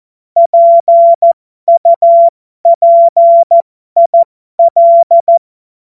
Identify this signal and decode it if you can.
Morse code, heard as a keyed tone, about 14 wpm, spaced as for 12 wpm, 680 Hz. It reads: PUPIL
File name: Morsekurs / Witze